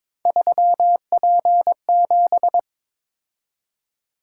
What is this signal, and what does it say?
Morse code, heard as a keyed tone, about 22 wpm, 700 Hz. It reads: 3P7
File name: practice / 22wpm / Three Letter-Number Combinations - 22wpm